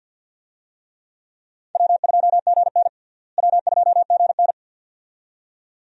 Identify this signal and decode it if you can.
Morse code, heard as a keyed tone, about 47 wpm, 700 Hz. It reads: W2GN W2GN
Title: USA Call Signs - 4 Characters - Repeated 2x - 50wpm